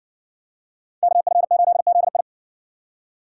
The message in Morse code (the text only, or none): KF8ZI